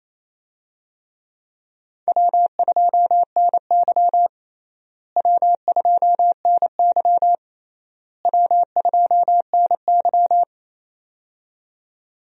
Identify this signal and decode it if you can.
Morse code, heard as a keyed tone, about 28 wpm, 700 Hz. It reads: W2NY W2NY W2NY
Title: USA Call Signs - 4 Characters - Repeated 3x - 28wpm